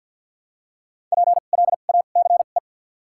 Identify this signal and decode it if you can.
Morse code, heard as a keyed tone, about 53 wpm, spaced as for 37 wpm, 700 Hz. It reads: PLACE
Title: 2x Character Spacing - Top 200 Words - 50wpm